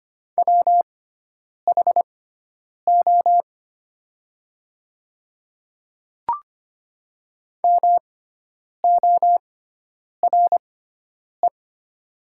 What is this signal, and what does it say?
Morse code, heard as a keyed tone, about 25 wpm, spaced as for 9 wpm, 700 Hz. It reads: WHO MORE
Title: 6x Character Spacing - Top 100 Words - 25wpm